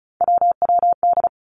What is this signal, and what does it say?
Morse code, heard as a keyed tone, about 36 wpm, 700 Hz. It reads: WWD
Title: Three Challenging Letter-Number Combinations 35wpm